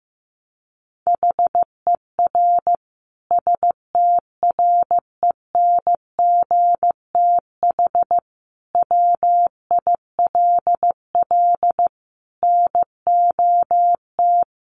Morse code, heard as a keyed tone, about 15 wpm, 700 Hz. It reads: HER STRENGTH WILL NOT
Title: Sentences from Top 1300 Words 15wpm